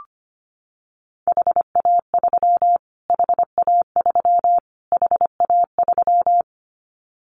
Morse code, heard as a keyed tone, about 25 wpm, 700 Hz. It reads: HA3 HA3 HA3
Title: Callsign Prefixes for Top 25 DXCC - Repeated 3x - 25wpm